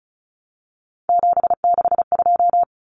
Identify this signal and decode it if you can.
Morse code, heard as a keyed tone, about 34 wpm, 700 Hz. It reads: 762
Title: Common Cut Numbers - 3-Digit 35wpm